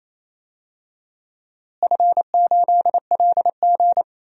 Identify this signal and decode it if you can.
Morse code, heard as a keyed tone, about 27 wpm, 700 Hz. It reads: F8LG
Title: DXCC-100 Call Signs - 4 Characters - Repeated 2x - 28wpm